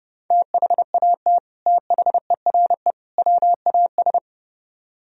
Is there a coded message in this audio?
Yes: Morse code from 0.3 to 4.2 s, heard as a keyed tone, about 30 wpm, 700 Hz. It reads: THAT THERE WAS